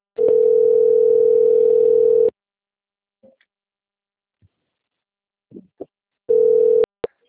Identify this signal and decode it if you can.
Ringback tone (North America), 440 + 480 Hz.